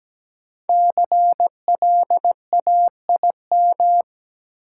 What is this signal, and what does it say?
Morse code, heard as a keyed tone, about 17 wpm, 700 Hz. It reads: CLAIM
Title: Sentences from Top 1500 Words - New Word Drill Alt - 17wpm